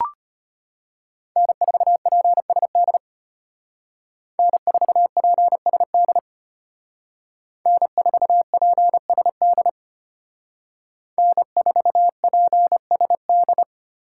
Morse code, heard as a keyed tone, about 32 wpm, 700 Hz. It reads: N4PSD N4PSD N4PSD N4PSD